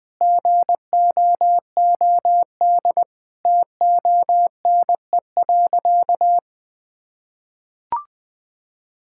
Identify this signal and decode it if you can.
Morse code, heard as a keyed tone, about 20 wpm, 700 Hz. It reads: GOOD TONE.
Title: Sentences from Top 1600 Words 20wpm